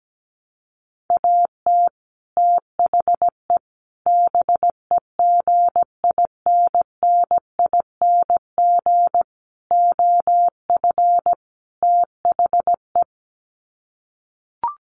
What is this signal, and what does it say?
Morse code, heard as a keyed tone, about 17 wpm, 700 Hz. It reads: AT THE BEGINNING OF THE
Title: Sets of 5 Words 17wpm